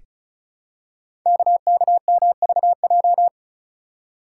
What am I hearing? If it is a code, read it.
Morse code, heard as a keyed tone, about 36 wpm, 700 Hz. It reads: KKMVJ
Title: Code Groups - 5 Letters - 35wpm